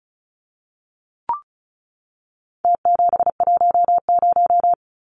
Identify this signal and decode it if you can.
Morse code, heard as a keyed tone, about 34 wpm, 700 Hz. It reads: T710
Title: Numbers - Contest Style - 4-Digit 35wpm